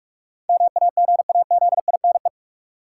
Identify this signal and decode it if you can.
Morse code, heard as a keyed tone, about 46 wpm, 700 Hz. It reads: MAGAZINE
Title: Sentences from Top 1000 Words - Review Part 7 - New Word Drill - 45wpm